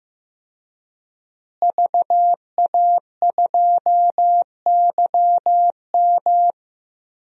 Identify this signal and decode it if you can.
Morse code, heard as a keyed tone, about 15 wpm, 700 Hz. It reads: VA2YM
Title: DXCC-100 Call Signs - 5 Characters - Repeated 3x - 15wpm